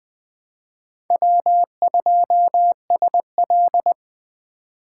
Morse code, heard as a keyed tone, about 24 wpm, 700 Hz. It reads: W2SL
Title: Speed Racing - USA Call Signs 20wpm